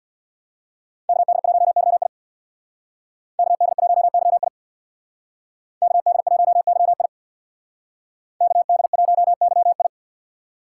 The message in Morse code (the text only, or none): KD1YI KD1YI KD1YI KD1YI